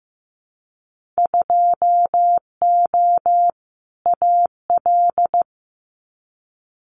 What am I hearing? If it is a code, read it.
Morse code, heard as a keyed tone, about 15 wpm, 700 Hz. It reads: 2O AL
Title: Winter Field Day Exchanges Encore 15wpm